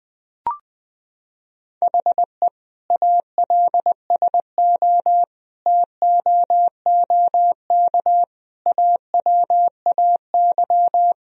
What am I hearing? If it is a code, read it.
Morse code, heard as a keyed tone, about 20 wpm, 700 Hz. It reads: HE ALSO TOOK AWAY
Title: Sentences from Top 1000 Words - Review Part 1 - 20wpm